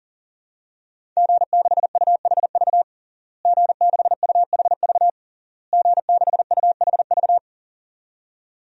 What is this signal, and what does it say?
Morse code, heard as a keyed tone, about 40 wpm, 700 Hz. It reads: G6UHV G6UHV G6UHV
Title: DXCC-100 Call Signs - 5 Characters - Repeated 3x - 40wpm